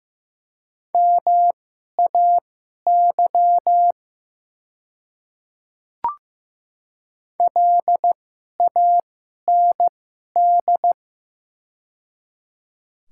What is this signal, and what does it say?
Morse code, heard as a keyed tone, about 15 wpm, spaced as for 11 wpm, 700 Hz. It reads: MAY LAND
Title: 2x Character Spacing - Top 300 Words - 15wpm